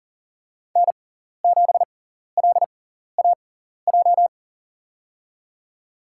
Morse code, heard as a keyed tone, about 38 wpm, spaced as for 14 wpm, 700 Hz. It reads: N7LAJ
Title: USA Call Signs - 5 Characters - 6x Character Spacing - 40wpm